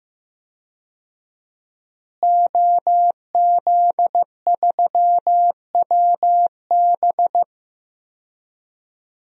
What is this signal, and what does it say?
Morse code, heard as a keyed tone, about 15 wpm, 700 Hz. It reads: OZ3WB
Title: DXCC-100 Call Signs - 5 Characters - Repeated 3x - 15wpm